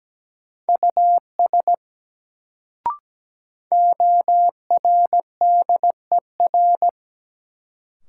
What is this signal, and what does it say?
Morse code, heard as a keyed tone, about 17 wpm, 700 Hz. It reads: US ORDER
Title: Sets of 2 Words - New Word Drill - 17wpm